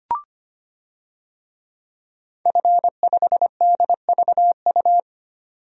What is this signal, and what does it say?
Morse code, heard as a keyed tone, about 25 wpm, 700 Hz. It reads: F5DVU
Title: DXCC-100 Call Signs - 5 Characters - 25wpm